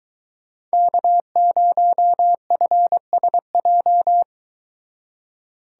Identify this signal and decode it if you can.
Morse code, heard as a keyed tone, about 23 wpm, 700 Hz. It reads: K0FSJ